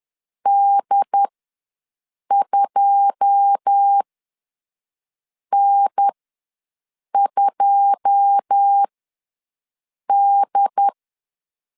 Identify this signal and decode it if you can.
Morse code, heard as a keyed tone, about 10 wpm, spaced as for 5 wpm, 800 Hz. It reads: D2N2D
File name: LEZIONI / 9 2 ws 10wpm 4130